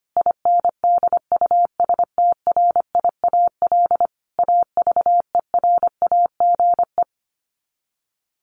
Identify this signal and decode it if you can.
Morse code, heard as a keyed tone, about 25 wpm, 700 Hz. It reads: INDUSTRIAL AVERAGE
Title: Sets of 5 Words 25wpm